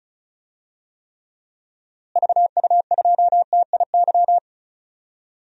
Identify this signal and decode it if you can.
Morse code, heard as a keyed tone, about 34 wpm, 700 Hz. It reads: VU2TIY
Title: DXCC-100 Call Signs - 6 Characters - Repeated 3x - 35wpm